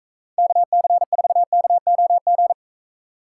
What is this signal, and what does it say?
Morse code, heard as a keyed tone, about 38 wpm, 700 Hz. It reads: KC4KOG